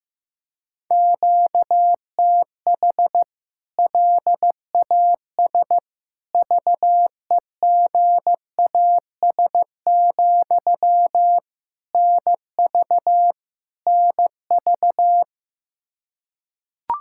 Morse code, heard as a keyed tone, about 15 wpm, 700 Hz. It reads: QTH LAS VEGAS, NV NV
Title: K1USN SST QSO Practice 15wpm